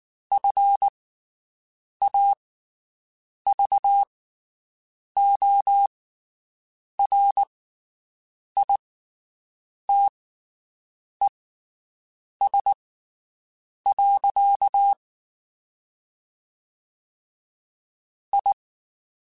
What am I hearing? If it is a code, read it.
Morse code, heard as a keyed tone, about 19 wpm, spaced as for 7 wpm, 800 Hz. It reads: FAVORITES. I
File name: QSOs Spaced